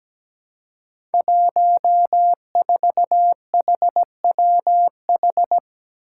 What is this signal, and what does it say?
Morse code, heard as a keyed tone, about 17 wpm, 700 Hz. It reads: 14HWH
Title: Five Challenging Letter-Number Combinations 17wpm